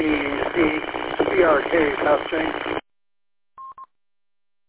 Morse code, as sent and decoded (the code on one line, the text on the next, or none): none